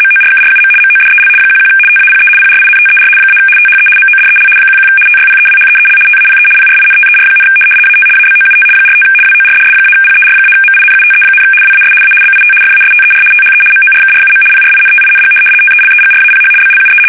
RTTY, about 75 baud. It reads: MORKRUM MODEL 11 TAPE PRINTER - THE MODEL 11 TYPEWHEEL TAPE PRINTER, AT ABOUT 45 WORDS-PER-MINUTE, WAS A BIT FASTER THE MORKRUM PRINTING TELEGRAPH BLUE
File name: fsk850-75B / teletype-corp